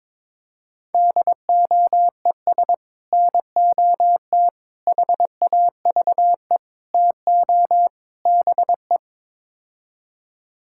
Morse code, heard as a keyed tone, about 23 wpm, 700 Hz. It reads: DOES NOT HAVE TO BE
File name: practice / 22wpm / Build A Phrase - 5-Word Phrases - 22wpm